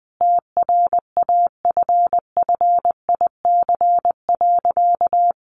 Morse code, heard as a keyed tone, about 20 wpm, 700 Hz. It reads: TRAFFIC.
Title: Sentences from Top 1300 Words 20wpm